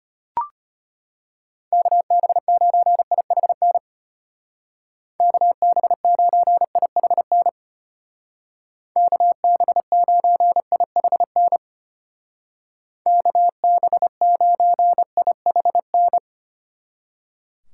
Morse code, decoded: KB9IHN KB9IHN KB9IHN KB9IHN

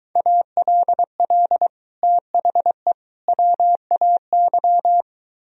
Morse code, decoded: ALL THE WAY

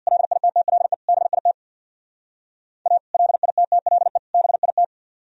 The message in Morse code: LITTLE BIT A LITTLE BIT

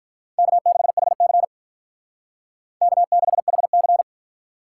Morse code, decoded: K6HC K6HC

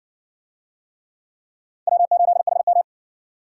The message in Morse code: W8HM

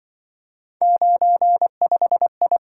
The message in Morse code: 95I

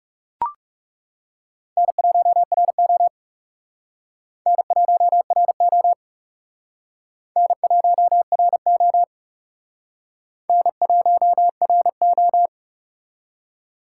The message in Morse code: N1RO N1RO N1RO N1RO

